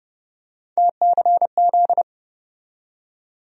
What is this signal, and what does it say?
Morse code, heard as a keyed tone, about 30 wpm, 700 Hz. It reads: TCZ
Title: USA Call Sign Postfixes - 3 Letters 30wpm